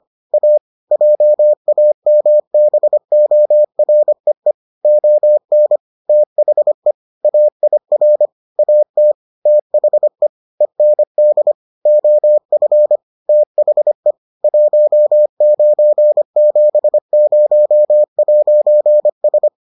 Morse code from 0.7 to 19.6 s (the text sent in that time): JAMBOREE ON THE AIR AT THE END OF THE 1970'S